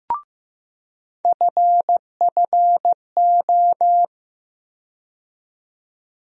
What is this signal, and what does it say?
Morse code, heard as a keyed tone, about 15 wpm, 700 Hz. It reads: FFO
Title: USA Call Sign Postfixes - 3 Letters 15wpm